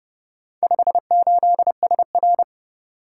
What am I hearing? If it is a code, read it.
Morse code, heard as a keyed tone, about 30 wpm, 700 Hz. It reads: 58SR